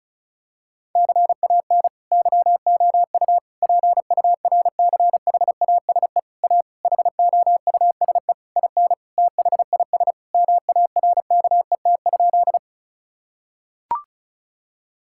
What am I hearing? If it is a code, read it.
Morse code, heard as a keyed tone, about 34 wpm, 700 Hz. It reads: CAN YOU PURCHASE A HOUSE IN THIS MARKET?
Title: Sentences from Top 1000 Words - Review Part 5 - 35wpm